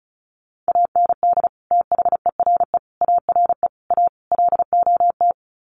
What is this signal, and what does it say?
Morse code, heard as a keyed tone, about 34 wpm, 700 Hz. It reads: AND THERE ARE A LOT
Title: Sets of 5 Words Encore2 35wpm